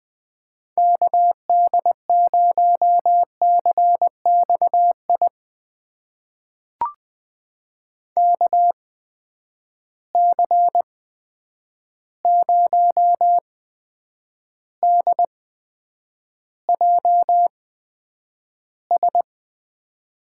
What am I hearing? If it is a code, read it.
Morse code, heard as a keyed tone, about 20 wpm, spaced as for 9 wpm, 700 Hz. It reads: KD0CXI KC0DJS
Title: USA Call Signs - 6 Characters - 8x Character Spacing - 20wpm